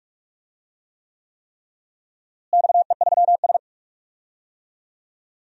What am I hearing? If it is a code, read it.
Morse code, heard as a keyed tone, about 46 wpm, 700 Hz. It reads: XE3S
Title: DXCC-100 Call Signs - 4 Characters - Repeated 3x - 45wpm